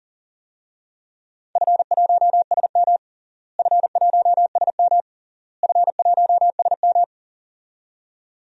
Morse code, heard as a keyed tone, about 40 wpm, 700 Hz. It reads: F1SM F1SM F1SM
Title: DXCC-100 Call Signs - 4 Characters - Repeated 3x - 40wpm